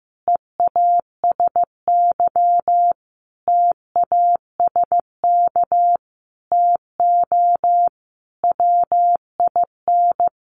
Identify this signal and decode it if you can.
Morse code, heard as a keyed tone, about 15 wpm, 700 Hz. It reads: EASY TASK TO WIN
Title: Sentences from Top 1500 Words 15wpm